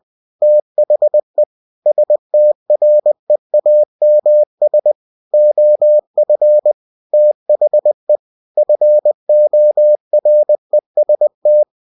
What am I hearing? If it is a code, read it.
Morse code, heard as a keyed tone, about 20 wpm, 600 Hz. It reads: THE STREAMS OF THE FOREST
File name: cw-books / winnie-the-pooh / winnie-20wpm-page-0101